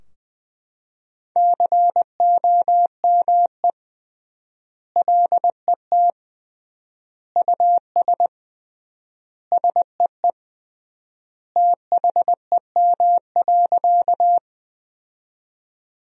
Morse code, heard as a keyed tone, about 20 wpm, 700 Hz. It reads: COME LET US SEE THEM.
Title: 3x Word-Spacing - Sentences from Top 300 Words - 20wpm